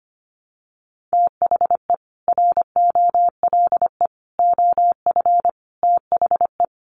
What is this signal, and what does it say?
Morse code, heard as a keyed tone, about 25 wpm, 700 Hz. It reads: THE ROLE OF THE
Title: Sets of 4 Words 25wpm